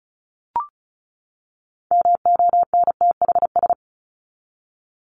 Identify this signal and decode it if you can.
Morse code, heard as a keyed tone, about 34 wpm, 700 Hz. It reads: MONTHS
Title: ES:S Suffix - Top 100 Words with Base Words 35wpm